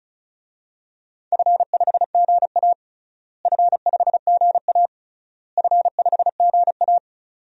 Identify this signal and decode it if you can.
Morse code, heard as a keyed tone, about 34 wpm, 700 Hz. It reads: F5GA F5GA F5GA
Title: DXCC-100 Call Signs - 4 Characters - Repeated 3x - 35wpm